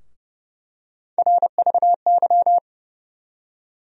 Morse code, heard as a keyed tone, about 30 wpm, 700 Hz. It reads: RVY
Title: USA Call Sign Postfixes - 3 Letters 30wpm